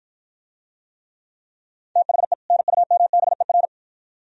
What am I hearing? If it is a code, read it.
Morse code, heard as a keyed tone, about 51 wpm, 700 Hz. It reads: THE NUMBER